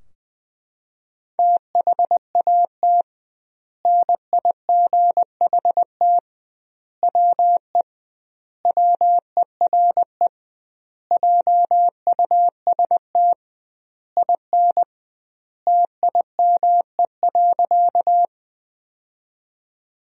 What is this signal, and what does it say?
Morse code, heard as a keyed tone, about 20 wpm, 700 Hz. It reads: THAT NIGHT WE WERE JUST IN TIME.
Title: 2x Word-Spacing - Sentences from Top 300 Words - 20wpm